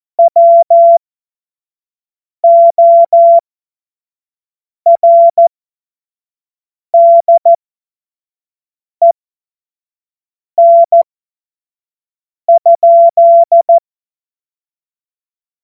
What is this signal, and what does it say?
Morse code, heard as a keyed tone, about 14 wpm, spaced as for 5 wpm, 680 Hz. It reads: WORDEN?
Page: cw-witze